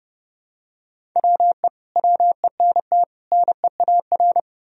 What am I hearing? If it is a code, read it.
Morse code, heard as a keyed tone, about 32 wpm, 700 Hz. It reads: WE WENT NEAR
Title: Sentences from Top 1000 Words - Review Part 1 - 30wpm